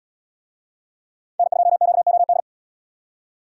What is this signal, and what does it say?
Morse code, heard as a keyed tone, about 76 wpm, 700 Hz. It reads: N2QOD